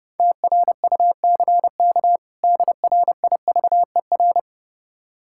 Morse code, heard as a keyed tone, about 30 wpm, 700 Hz. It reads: TRUCK DRIVER
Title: Practice QSO - Exchange 3 - More Info - 30wpm